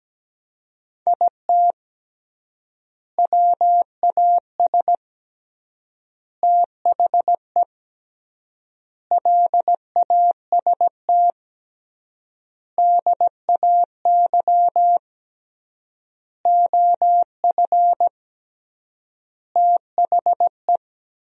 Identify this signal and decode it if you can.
Morse code, heard as a keyed tone, about 17 wpm, 700 Hz. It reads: IT WAS THE LAST DAY OF THE
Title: 3x Word-Spacing - Sentences from Top 300 Words - 17wpm